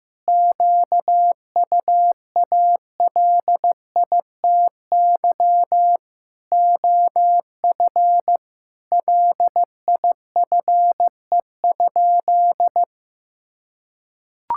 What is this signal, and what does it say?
Morse code, heard as a keyed tone, about 15 wpm, 700 Hz. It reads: QUALITY OF LIFE?
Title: Sentences from Top 1000 Words - Review Part 5 - 15wpm